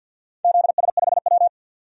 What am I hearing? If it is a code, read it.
Morse code, heard as a keyed tone, about 46 wpm, 700 Hz. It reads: ZS5W
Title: DXCC-100 Call Signs - 4 Characters - 50wpm